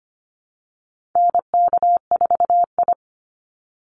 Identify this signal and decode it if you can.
Morse code, heard as a keyed tone, about 25 wpm, 700 Hz. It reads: NK4I
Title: Call Signs Repeated 3x - 25wpm